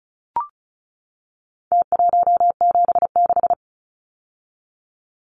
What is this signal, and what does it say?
Morse code, heard as a keyed tone, about 34 wpm, 700 Hz. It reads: T176
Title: Numbers - Contest Style - 4-Digit 35wpm